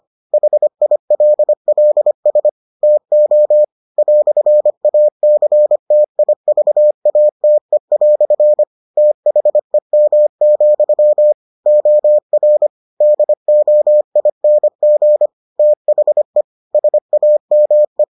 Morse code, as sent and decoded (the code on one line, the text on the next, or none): .... .. .-.. .-.. ... / - --- / .-..-. .- -.-. - .. ...- .- - . .-..-. / - .... -- --..-- / --- .-. / -.. --- .. -. --. / - .... / ... .- -- .
HILLS TO "ACTIVATE" THEM, OR DOING THE SAME